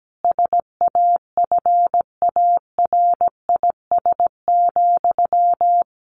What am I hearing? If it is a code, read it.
Morse code, heard as a keyed tone, about 17 wpm, 700 Hz. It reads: SAFARIS,
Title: A Sound Of Thunder 17wpm